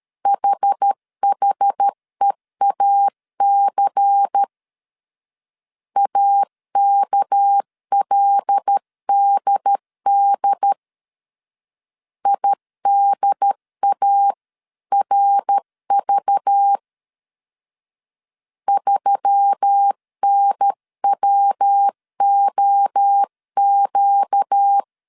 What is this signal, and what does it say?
Morse code, heard as a keyed tone, about 13 wpm, 800 Hz. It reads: HHEAC AKLDD IDARV 3NWOQ
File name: CW / DET / 1 6 1500C S C 12wpm 3533